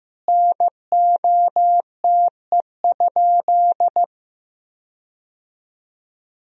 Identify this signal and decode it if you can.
Morse code, heard as a keyed tone, about 15 wpm, 700 Hz. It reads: NOTE?
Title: 3x Word-Spacing - Sentences from Top 500 Words - 15wpm